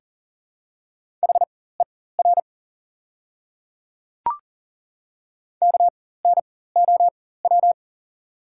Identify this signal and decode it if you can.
Morse code, heard as a keyed tone, about 40 wpm, spaced as for 19 wpm, 700 Hz. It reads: HER KNOW
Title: 4x Character Spacing - Top 300 Words - 40wpm